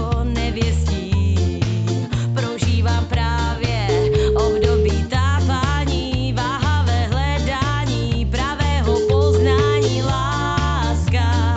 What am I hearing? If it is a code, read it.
Ringback tone (Europe, 425 Hz), with noise.